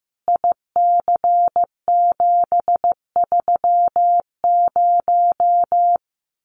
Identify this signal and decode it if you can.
Morse code, heard as a keyed tone, about 15 wpm, 700 Hz. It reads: IC730
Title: Common QSO Phrases 15wpm